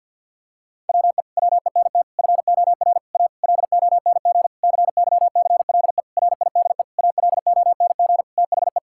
Morse code, heard as a keyed tone, about 50 wpm, 700 Hz. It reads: WE WENT FOR A LONG CYCLE RIDE ALONG THE
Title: Sentences from Top 1000 Words - Review Part 6 - 50wpm